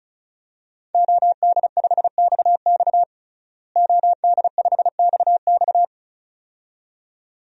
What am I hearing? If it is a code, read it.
Morse code, heard as a keyed tone, about 36 wpm, 700 Hz. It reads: OD5XX OD5XX
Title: DXCC-100 Call Signs - 5 Characters - Repeated 2x - 35wpm